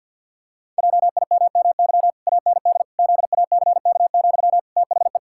Morse code, heard as a keyed tone, about 53 wpm, 700 Hz. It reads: JIMMY AND ZACK, THE
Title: Pangram Challenge v2 - 50wpm